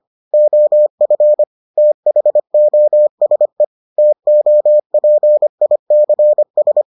Morse code, heard as a keyed tone, about 25 wpm, 600 Hz. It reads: OF THOSE TOPICS